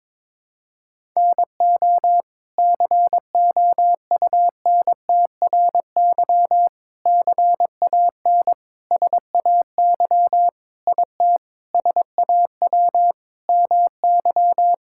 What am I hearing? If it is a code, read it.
Morse code, heard as a keyed tone, about 22 wpm, 700 Hz. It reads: NO COUNTRY CAN SAY IT SAW MY
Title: Sentences from Top 1000 Words - Review Part 5 - 22wpm